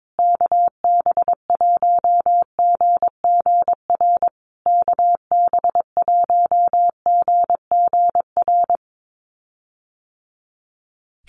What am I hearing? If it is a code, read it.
Morse code, heard as a keyed tone, about 22 wpm, 700 Hz. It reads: KB1GGR KB1GGR
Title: USA Call Signs Repeated 3x 22wpm